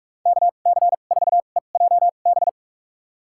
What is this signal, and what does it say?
Morse code, heard as a keyed tone, about 47 wpm, spaced as for 33 wpm, 700 Hz. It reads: KC4EJB
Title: USA Call Signs - 2x Character Spacing - 45wpm